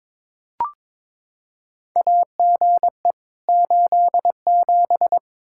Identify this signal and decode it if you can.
Morse code, heard as a keyed tone, about 21 wpm, 700 Hz. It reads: AGE 87
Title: K1USN SST QSO Practice 22wpm